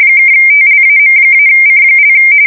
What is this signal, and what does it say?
RTTY, about 45.45 baud. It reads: (TLAM-C)
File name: fsk170 / battleshipnj